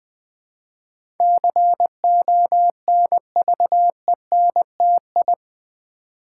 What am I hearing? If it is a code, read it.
Morse code, heard as a keyed tone, about 20 wpm, 700 Hz. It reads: CONVENTI
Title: Build A Word - 4 Letters At A Time - 12-Letter Words - 20wpm